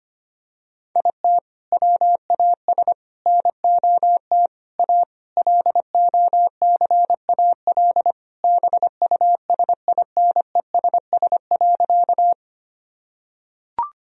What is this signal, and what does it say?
Morse code, heard as a keyed tone, about 25 wpm, 700 Hz. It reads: IT WAS NOT A LOCAL BUSINESS.
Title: Sentences from Top 1000 Words - Review Part 3 - 25wpm